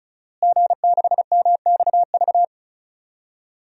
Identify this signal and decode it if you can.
Morse code, heard as a keyed tone, about 36 wpm, 700 Hz. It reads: G6MXV
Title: DXCC-100 Call Signs - 5 Characters - Repeated 3x - 35wpm